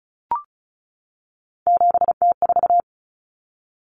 Morse code, heard as a keyed tone, about 34 wpm, 700 Hz. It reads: 7T4
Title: Common Cut Numbers - 3-Digit 35wpm